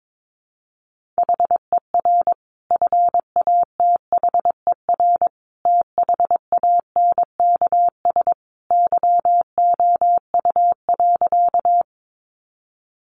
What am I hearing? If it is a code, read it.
Morse code, heard as a keyed tone, about 22 wpm, 700 Hz. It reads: HER FATHER THANKS YOU.
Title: Sentences from Top 600 Words 22wpm